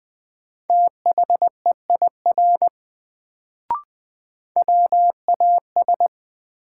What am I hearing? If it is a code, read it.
Morse code, heard as a keyed tone, about 21 wpm, 700 Hz. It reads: THEIR WAS
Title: Sets of 2 Words - New Word Drill - 20wpm